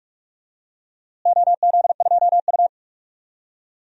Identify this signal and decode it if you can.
Morse code, heard as a keyed tone, about 46 wpm, 700 Hz. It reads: OZ2U
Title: DXCC-100 Call Signs - 4 Characters - Repeated 2x - 45wpm